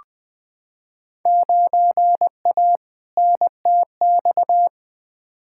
Morse code, heard as a keyed tone, about 20 wpm, 700 Hz. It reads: 9A NTX
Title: Field Day Exchanges Encore - 20wpm